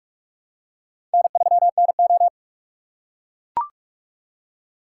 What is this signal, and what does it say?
Morse code, heard as a keyed tone, about 43 wpm, 700 Hz. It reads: N3NO
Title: USA Call Signs - 4 Characters - 2x Character Spacing - 45wpm